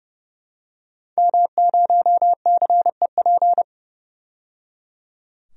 Morse code, heard as a keyed tone, about 31 wpm, 700 Hz. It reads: M0CEP
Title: DXCC-100 Call Signs - 5 Characters - 30wpm